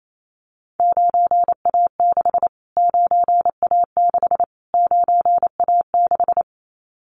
Morse code, heard as a keyed tone, about 28 wpm, 700 Hz. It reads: 9A6 9A6 9A6
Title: Callsign Prefixes for Top 25 DXCC - Repeated 3x - 28wpm